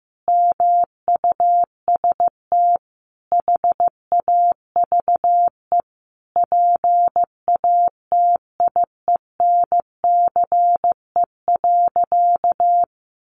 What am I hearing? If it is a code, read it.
Morse code, heard as a keyed tone, about 15 wpm, 700 Hz. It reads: MUST HAVE PATIENCE.
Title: Sentences from Top 1300 Words 15wpm